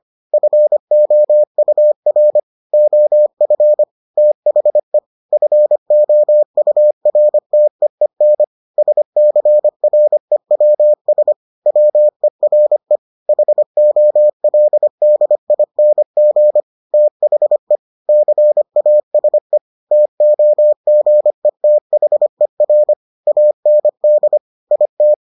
Morse code, heard as a keyed tone, about 25 wpm, 600 Hz. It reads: FOUR OF THE FOURTEEN SCREWS WERE HOLDING THE CASE TOGETHER AND IT